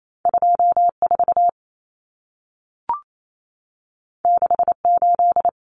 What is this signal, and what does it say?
Morse code, heard as a keyed tone, about 28 wpm, 700 Hz. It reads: 24 68
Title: Numbers - 2-Digit 28wpm